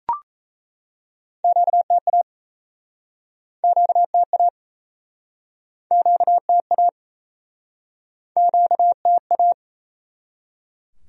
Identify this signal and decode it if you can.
Morse code, heard as a keyed tone, about 39 wpm, 700 Hz. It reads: QTA QTA QTA QTA